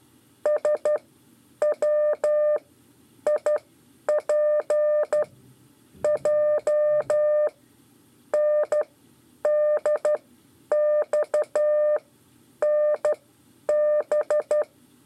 Morse code, heard as a keyed tone, about 12 wpm, spaced as for 8 wpm, 600 Hz. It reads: SWIPJNDXNB
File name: 5 LEZIONE RIPASSO 1